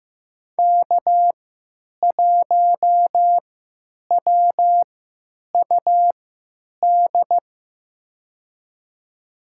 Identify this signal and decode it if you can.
Morse code, heard as a keyed tone, about 15 wpm, spaced as for 9 wpm, 700 Hz. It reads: K1WUD
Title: USA Call Signs - 5 Characters - 3x Character Spacing - 15wpm